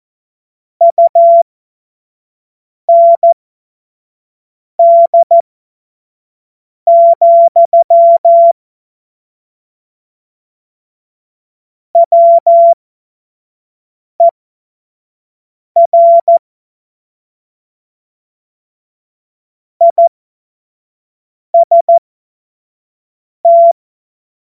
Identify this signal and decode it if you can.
Morse code, heard as a keyed tone, about 14 wpm, spaced as for 5 wpm, 680 Hz. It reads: UND, WER IST
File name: Morsekurs / Witze / Witz 31. August 2021, 25 F70